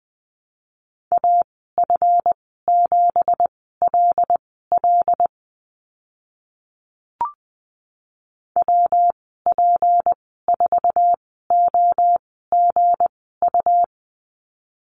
Morse code, heard as a keyed tone, about 20 wpm, spaced as for 15 wpm, 700 Hz. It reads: AF7LL WP4OGU
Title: 186 Lesson 30g - Letter b - Call Signs 20wpm